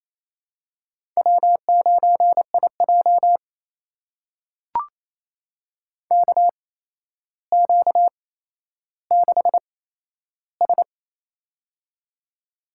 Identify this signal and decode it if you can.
Morse code, heard as a keyed tone, about 27 wpm, spaced as for 12 wpm, 700 Hz. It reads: W9IJ KQ6S
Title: USA Call Signs - 4 Characters - 8x Character Spacing - 28wpm